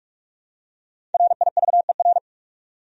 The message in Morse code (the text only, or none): RIVER